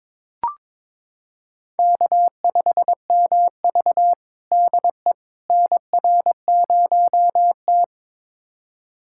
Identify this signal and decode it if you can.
Morse code, heard as a keyed tone, about 22 wpm, 700 Hz. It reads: K5MV DE NR0T
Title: USA Callsign DE Callsign - 4 Characters - 22wpm